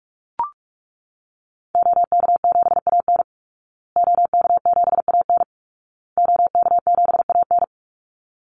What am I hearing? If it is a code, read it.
Morse code, heard as a keyed tone, about 44 wpm, 700 Hz. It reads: OK7AN OK7AN OK7AN
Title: Call Signs Repeated 2x - 45wpm